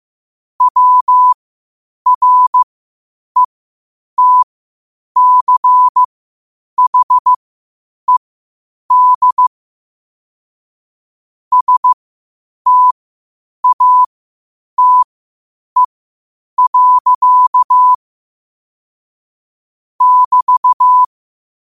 Morse code, heard as a keyed tone, about 15 wpm, spaced as for 8 wpm, 1000 Hz. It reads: WRETCHED STATE. <BT>
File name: qotdfiles / 8wpm